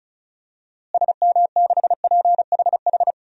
Morse code, heard as a keyed tone, about 36 wpm, 700 Hz. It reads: SM6PHH
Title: DXCC-100 Call Signs - 6 Characters - Repeated 2x - 35wpm